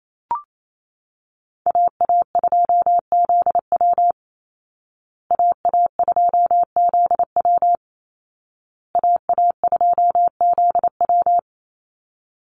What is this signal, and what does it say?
Morse code, heard as a keyed tone, about 28 wpm, 700 Hz. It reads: AA2ZW AA2ZW AA2ZW
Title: Call Signs Repeated 2x - 28wpm